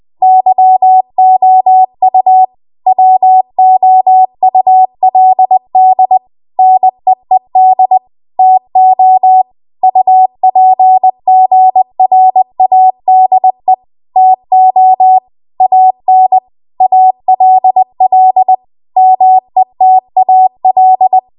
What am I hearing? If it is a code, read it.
Morse code, heard as a keyed tone, about 20 wpm, 750 Hz. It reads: YOU WOULD NEED TO UPGRADE TO AN ALL METAL
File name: Archive / 20 WPM / 20WPM